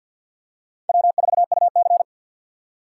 Morse code, heard as a keyed tone, about 51 wpm, 700 Hz. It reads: W4UC
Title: USA Call Signs - 4x Character Spacing - 50wpm